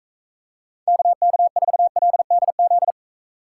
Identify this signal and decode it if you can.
Morse code, heard as a keyed tone, about 40 wpm, 700 Hz. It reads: KK4LDZ